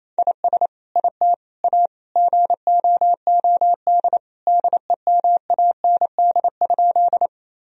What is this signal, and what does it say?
Morse code, heard as a keyed tone, about 28 wpm, 700 Hz. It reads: IS IT A GOOD DEMAND?
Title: Sentences from Top 1000 Words - Review Part 4 - 28wpm